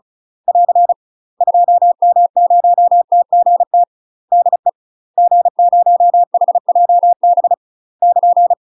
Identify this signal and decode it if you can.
Morse code, heard as a keyed tone, about 36 wpm, 700 Hz. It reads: <AR> 2M0TGT DE G0HJB <KN>
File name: UK MASTER QSOs TRAINING / 35WPM / UK MASTER QSOs 0011